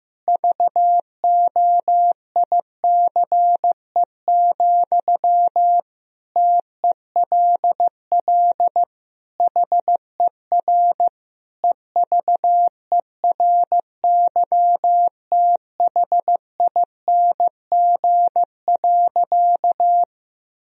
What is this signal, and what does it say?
Morse code, heard as a keyed tone, about 15 wpm, 700 Hz. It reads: VOICE, TELL HER EVERYTHING.
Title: Sentences from Top 1000 Words - Review Part 1 - 15wpm